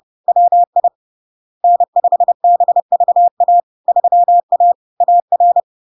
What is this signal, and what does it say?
Morse code, heard as a keyed tone, about 30 wpm, 700 Hz. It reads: WI N5BVA 3A AR
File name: LARK-FD-2022-Log-in-CW-30wpm-700-Hz